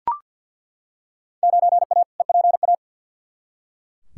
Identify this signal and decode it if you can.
Morse code, heard as a keyed tone, about 49 wpm, 700 Hz. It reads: 9A EPA
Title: Field Day Exchanges Encore - 50wpm